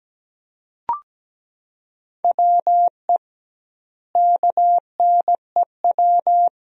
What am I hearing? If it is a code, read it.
Morse code, heard as a keyed tone, about 17 wpm, 700 Hz. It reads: WE KNEW
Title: 2x Word-Spacing - Sentences from Top 400 Words - 17wpm